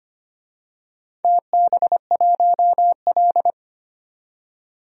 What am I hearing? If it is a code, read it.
Morse code, heard as a keyed tone, about 25 wpm, 700 Hz. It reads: TB1L
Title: DXCC-100 Call Signs - 4 Characters - 25wpm